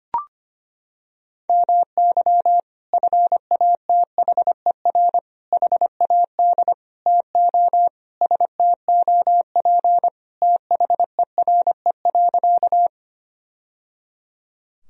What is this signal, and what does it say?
Morse code, heard as a keyed tone, about 25 wpm, 700 Hz. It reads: MY FATHER HAD TO STOP THERE.
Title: Sentences from Top 1000 Words - Review Part 1 - 25wpm